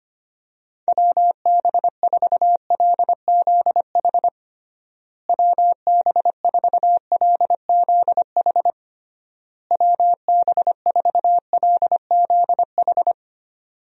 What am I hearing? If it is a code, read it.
Morse code, heard as a keyed tone, about 25 wpm, 700 Hz. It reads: WB4LZH WB4LZH WB4LZH
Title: USA Call Signs - 6 Characters - Repeated 3x - 25wpm